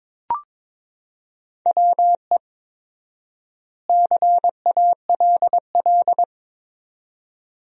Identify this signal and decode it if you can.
Morse code, heard as a keyed tone, about 22 wpm, 700 Hz. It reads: WE CALL
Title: 4x Word-Spacing - Sentences from Top 400 Words - 22wpm